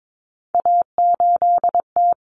Morse code, heard as a keyed tone, about 22 wpm, 700 Hz. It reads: A8T
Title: Common Cut Numbers - 3-Digit 22wpm